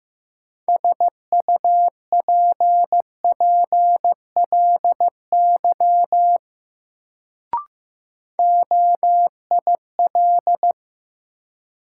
Morse code, heard as a keyed tone, about 15 wpm, 700 Hz. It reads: SUPPLY OIL
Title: Sentences from Top 1000 Words - Review Part 4 - New Word Drill - 15wpm